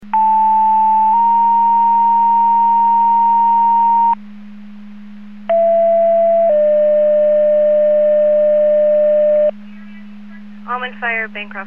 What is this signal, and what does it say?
Two-tone page, heard as a one-second tone, then a three-second tone, about 880 then 930 Hz, about 670 then 600 Hz, with speech. It is heard from 0.1 to 9.5 s.